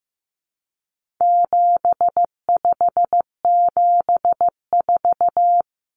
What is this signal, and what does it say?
Morse code, heard as a keyed tone, about 15 wpm, 700 Hz. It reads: 7574
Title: Common Cut Numbers - 4-Digit 15wpm